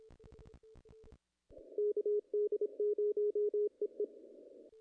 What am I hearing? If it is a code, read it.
Morse code, heard as a keyed tone, about 25 wpm, 420 Hz. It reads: KD0EE
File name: contests / ss-cw-k5tr-op-n5rz -2300